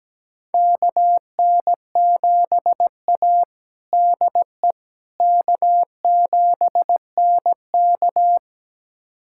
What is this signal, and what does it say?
Morse code, heard as a keyed tone, about 17 wpm, 700 Hz. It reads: KN7A DE K7NK
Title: USA Callsign DE Callsign - 4 Characters - 17wpm